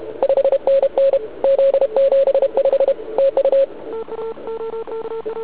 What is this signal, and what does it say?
Morse code, heard as a keyed tone, about 32 wpm, 550 Hz. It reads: 5NN Z75 TU
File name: ARCHIV / zvuk2 / CW RUN